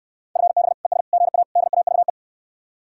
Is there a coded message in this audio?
Yes: Morse code from 0.4 to 2.1 s, heard as a keyed tone, about 66 wpm, 700 Hz. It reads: FB ES GA DAVE